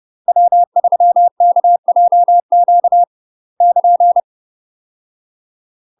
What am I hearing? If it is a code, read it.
Morse code, heard as a keyed tone, about 30 wpm, 700 Hz. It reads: W3KJQ <KN>
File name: UK MASTER QSOs TRAINING / 30WPM / UK MASTER QSOs 0009